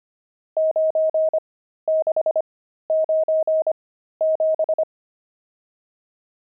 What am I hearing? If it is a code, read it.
Morse code, heard as a keyed tone, about 25 wpm, spaced as for 13 wpm, 630 Hz. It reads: 9697
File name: pangram-fs-13-cs-25-nums4